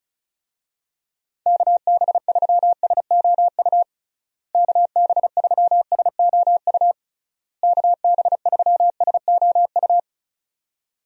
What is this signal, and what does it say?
Morse code, heard as a keyed tone, about 36 wpm, 700 Hz. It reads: KB3SOU KB3SOU KB3SOU
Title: USA Call Signs - 6 Characters - Repeated 3x - 35wpm